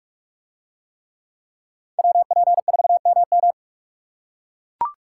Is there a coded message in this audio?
Yes: Morse code from 2.0 to 3.5 s, heard as a keyed tone, about 42 wpm, 700 Hz. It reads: WP4MM